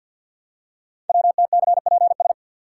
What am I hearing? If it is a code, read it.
Morse code, heard as a keyed tone, about 50 wpm, 700 Hz. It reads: WTCPS
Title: Code Groups - 5 Letters - 50wpm